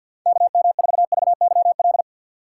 Morse code, heard as a keyed tone, about 52 wpm, 700 Hz. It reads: KM4VYL
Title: USA Call Signs - 6 Characters - 6x Character Spacing - 50wpm